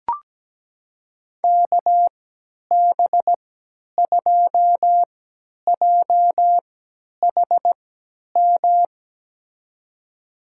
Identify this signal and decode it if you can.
Morse code, heard as a keyed tone, about 17 wpm, spaced as for 10 wpm, 700 Hz. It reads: KB2JHM
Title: USA Call Signs - 6 Characters - 3x Character Spacing - 17wpm